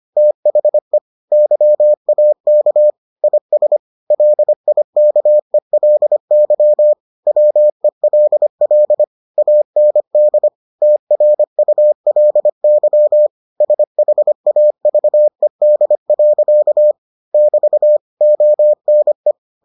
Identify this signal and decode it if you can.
Morse code, heard as a keyed tone, about 25 wpm, 600 Hz. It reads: THE YAK IS LIKELY WELL AND TRULY SHAVED. <BT> ONE